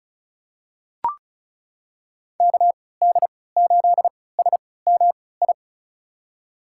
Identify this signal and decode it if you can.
Morse code, heard as a keyed tone, about 36 wpm, spaced as for 20 wpm, 690 Hz. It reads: KD8SMI
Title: USA Call Signs - 6 Characters - 3x Character Spacing - 35wpm